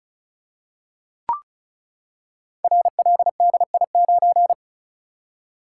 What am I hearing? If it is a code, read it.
Morse code, heard as a keyed tone, about 36 wpm, 700 Hz. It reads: RLDI9
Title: Code Groups - 5 Letter:Numbers - 35wpm